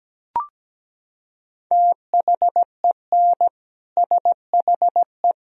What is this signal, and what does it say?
Morse code, heard as a keyed tone, about 17 wpm, 700 Hz. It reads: THEN SHE
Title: Sentences from Top 400 Words and Abbreviations 17wpm